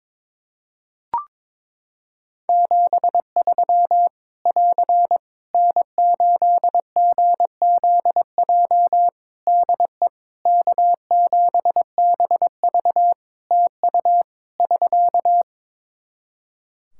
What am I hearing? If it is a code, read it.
Morse code, heard as a keyed tone, about 22 wpm, 700 Hz. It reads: 73 <AR> N8GZJ DE K7BV TU <SK>
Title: Practice QSO - Exchange 4 - Ending - 22wpm